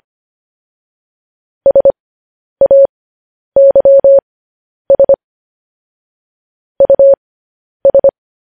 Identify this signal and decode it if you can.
Morse code, heard as a keyed tone, about 25 wpm, spaced as for 10 wpm, 550 Hz. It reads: SAYS US
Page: news headlines in morse code at 10 wpm